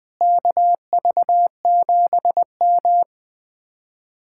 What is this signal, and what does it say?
Morse code, heard as a keyed tone, about 20 wpm, 700 Hz. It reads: KV7M
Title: Speed-Racing Common QSO Phrases - 20wpm